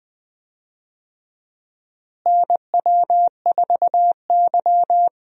Morse code, heard as a keyed tone, about 20 wpm, 700 Hz. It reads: NW4Y
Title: USA Call Signs - 4 Characters - Repeated 2x - 20wpm